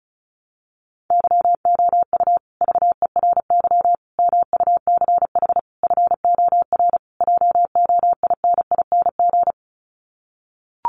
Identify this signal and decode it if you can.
Morse code, heard as a keyed tone, about 34 wpm, 700 Hz. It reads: YOU VERY MUCH FOR JOINING
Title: Sets of 5 Words Encore 35wpm